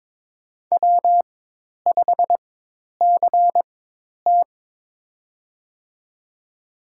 Morse code, heard as a keyed tone, about 22 wpm, spaced as for 10 wpm, 700 Hz. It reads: W5CT
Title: USA Call Signs - 4 Characters - 4x Character Spacing - 22wpm